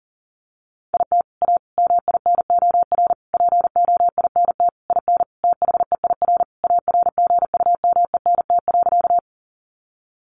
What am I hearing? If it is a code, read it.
Morse code, heard as a keyed tone, about 40 wpm, 700 Hz. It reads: IT A MINOR POINT IN THEIR ARGUMENT.
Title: Sentences from Top 1500 Words 40wpm